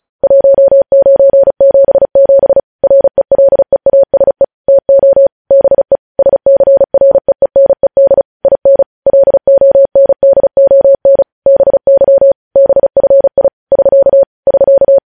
Morse code, heard as a keyed tone, about 36 wpm, 550 Hz. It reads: 1977 RELEASE TO BE SCREENED IN LONDON BY BFI <SK> <SK>
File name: podcast / News headlines-1x-35x35